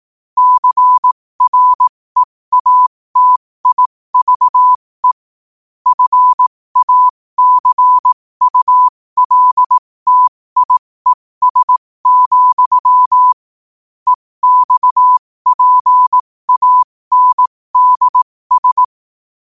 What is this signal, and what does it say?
Morse code, heard as a keyed tone, about 18 wpm, spaced as for 15 wpm, 1000 Hz. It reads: CREATIVE FACULTIES, EXPANDS
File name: 15wpm